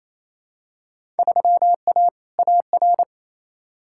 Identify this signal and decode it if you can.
Morse code, heard as a keyed tone, about 28 wpm, 700 Hz. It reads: 3A AR
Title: Field Day Exchanges Encore - 28wpm